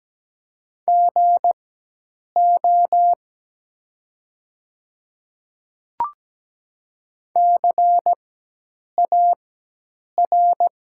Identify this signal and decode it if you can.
Morse code, heard as a keyed tone, about 17 wpm, spaced as for 8 wpm, 700 Hz. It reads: GO CAR
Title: 4x Character Spacing - Top 300 Words - 17wpm